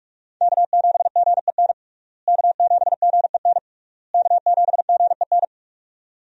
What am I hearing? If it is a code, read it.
Morse code, heard as a keyed tone, about 41 wpm, 700 Hz. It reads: K7GEN K7GEN K7GEN
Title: USA Call Signs - 5 Characters - Repeated 3x - 45wpm